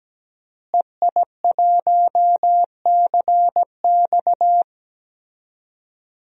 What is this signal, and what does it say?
Morse code, heard as a keyed tone, about 17 wpm, 700 Hz. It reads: EI1CX
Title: DXCC-100 Call Signs - 5 Characters - Repeated 2x - 17wpm